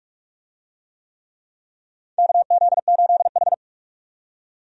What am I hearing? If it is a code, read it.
Morse code, heard as a keyed tone, about 42 wpm, 700 Hz. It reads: KZ8H
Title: USA Call Signs - 4 Characters - 45wpm